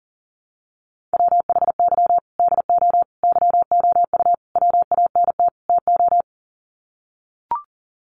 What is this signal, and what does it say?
Morse code, heard as a keyed tone, about 40 wpm, 700 Hz. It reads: WHY DO YOU WANT TO
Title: Sets of 5 Words Encore2 40wpm